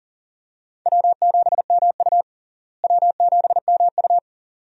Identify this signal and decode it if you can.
Morse code, heard as a keyed tone, about 40 wpm, 700 Hz. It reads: W7MU W7MU
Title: USA Call Signs - 4 Characters - Repeated 2x - 40wpm